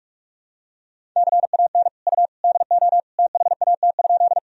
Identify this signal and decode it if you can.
Morse code, heard as a keyed tone, about 46 wpm, 700 Hz. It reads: CAN U DO THAT?
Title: Sentences from Top 100 Words and Abbreviations 45wpm